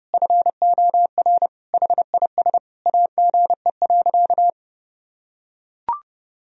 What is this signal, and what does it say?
Morse code, heard as a keyed tone, about 30 wpm, 700 Hz. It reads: FOR HIS AGE.
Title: Sentences from Top 1000 Words - Review Part 3 - 30wpm